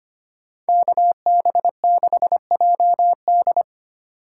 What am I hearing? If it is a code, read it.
Morse code, heard as a keyed tone, about 31 wpm, 700 Hz. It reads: KB6JD